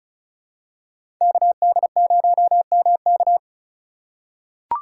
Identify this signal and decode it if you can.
Morse code, heard as a keyed tone, about 34 wpm, 700 Hz. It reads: KD0MK
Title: USA Call Signs - 5 Characters - 3x Character Spacing - 35wpm